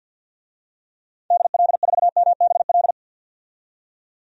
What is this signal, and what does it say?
Morse code, heard as a keyed tone, about 52 wpm, 700 Hz. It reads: DL4MBL
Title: DXCC-100 Call Signs - 6 Characters - 50wpm